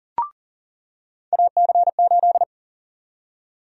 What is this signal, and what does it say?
Morse code, heard as a keyed tone, about 40 wpm, 700 Hz. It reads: AC8
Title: USA Call Sign Prefixes - 40wpm